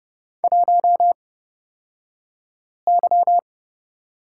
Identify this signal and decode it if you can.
Morse code, heard as a keyed tone, about 30 wpm, spaced as for 5 wpm, 700 Hz. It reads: 1Y
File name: Lesson 37d - Letter x - Two Characters 30wpm